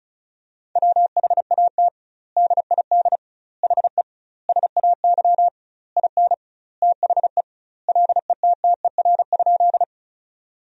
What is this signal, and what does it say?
Morse code, heard as a keyed tone, about 36 wpm, 700 Hz. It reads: WHAT DID HE SAY IN THE LETTER?
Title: 2x Word-Spacing - Sentences from Top 300 Words - 35wpm